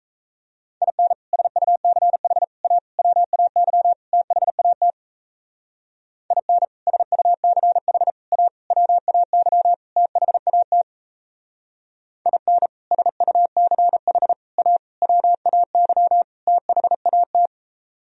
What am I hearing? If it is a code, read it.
Morse code, heard as a keyed tone, about 34 wpm, 700 Hz. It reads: IN SUCH A WAY THAT IN SUCH A WAY THAT IN SUCH A WAY THAT
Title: Speed-Racing - Sets of 5 Words 28wpm